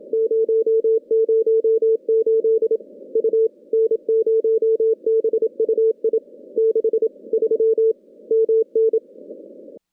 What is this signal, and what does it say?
Morse code, heard as a keyed tone, about 27 wpm, 440 Hz. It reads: 008 U N0BUI 63 MN
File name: contests / CW-SS-k5tr- -2100